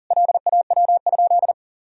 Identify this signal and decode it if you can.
Morse code, heard as a keyed tone, about 40 wpm, 700 Hz. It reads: LAW?
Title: Sentences from Top 1000 Words - Review Part 4 - 40wpm